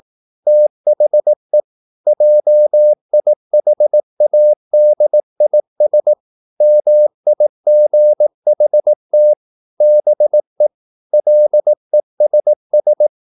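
Morse code, heard as a keyed tone, about 18 wpm, 600 Hz. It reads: THE JIHADIS MIGHT BE LESS
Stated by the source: Morse practice at 18 wpm, English text, from CW med Gnister.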